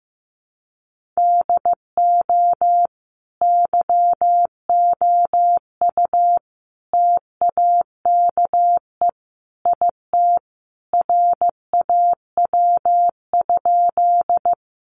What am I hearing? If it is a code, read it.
Morse code, heard as a keyed tone, about 15 wpm, 700 Hz. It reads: DO YOU TAKE IT RAW?
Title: Sentences from Top 1500 Words 15wpm